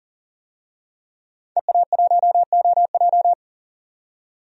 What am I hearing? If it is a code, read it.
Morse code, heard as a keyed tone, about 40 wpm, 700 Hz. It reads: EA1OJ
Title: DXCC-100 Call Signs - 5 Characters - 40wpm